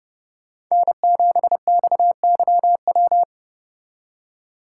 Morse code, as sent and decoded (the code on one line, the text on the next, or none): -. --... -..- -.-- .--
N7XYW